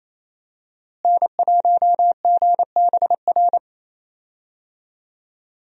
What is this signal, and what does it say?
Morse code, heard as a keyed tone, about 28 wpm, 700 Hz. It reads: N1GBR
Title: USA Call Signs - 5 Characters - Repeated 3x - 28wpm